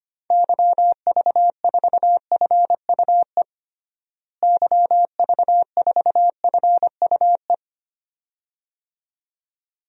Morse code, heard as a keyed tone, about 25 wpm, 700 Hz. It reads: YV4FUE YV4FUE
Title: DXCC-100 Call Signs - 6 Characters - Repeated 2x - 25wpm